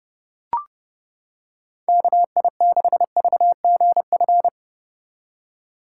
Morse code, heard as a keyed tone, about 30 wpm, 700 Hz. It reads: KI6VGF